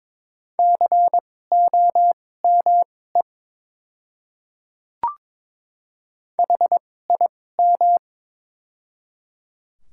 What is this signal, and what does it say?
Morse code, heard as a keyed tone, about 22 wpm, spaced as for 16 wpm, 700 Hz. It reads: COME HIM
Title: 2x Character Spacing - Top 100 Words - 22wpm